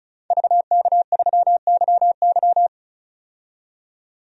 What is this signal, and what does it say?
Morse code, heard as a keyed tone, about 36 wpm, 700 Hz. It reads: VK3YY
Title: DXCC-100 Call Signs - 5 Characters - 35wpm